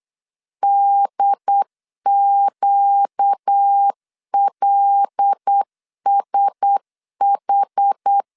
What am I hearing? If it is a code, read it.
Morse code, heard as a keyed tone, about 8 wpm, 800 Hz. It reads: DQLSH
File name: LEZIONI / 10 SIGNS / 10D COMA 8wpm 3749